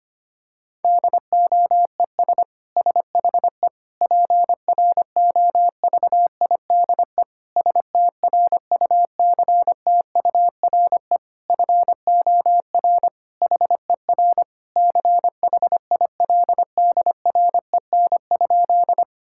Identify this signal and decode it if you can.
Morse code, heard as a keyed tone, about 25 wpm, 700 Hz. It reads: DOES SHE PROVIDE STRUCTURE FOR HER CHILDREN?
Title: Sentences from Top 1000 Words - Review Part 6 - 25wpm